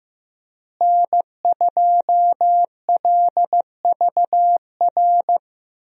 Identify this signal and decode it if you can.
Morse code, heard as a keyed tone, about 15 wpm, 700 Hz. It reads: N2LVR
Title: USA Call Signs - 5 Characters - Repeated 3x - 15wpm